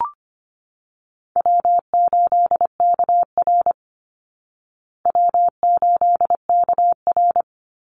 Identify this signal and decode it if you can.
Morse code, heard as a keyed tone, about 25 wpm, 700 Hz. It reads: W8KR W8KR
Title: Easy Call Signs Repeated 3x - 25wpm